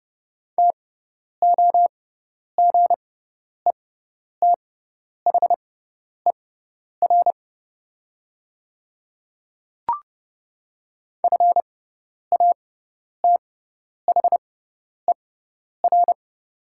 Morse code, heard as a keyed tone, about 30 wpm, spaced as for 10 wpm, 690 Hz. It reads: TOGETHER FATHER
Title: 6x Character Spacing - Top 300 Words - 30wpm